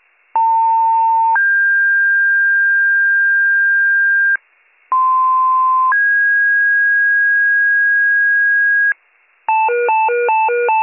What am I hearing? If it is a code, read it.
Two-tone page, heard as a one-second tone, then a three-second tone, about 900 then 1600 Hz, about 1040 then 1650 Hz. It is heard from 0.3 to 8.9 s.